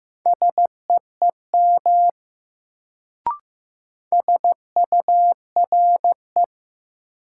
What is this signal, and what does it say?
Morse code, heard as a keyed tone, about 16 wpm, 700 Hz. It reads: SEEM SURE
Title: Sentences from Top 300 Words - New Word Drill - 15wpm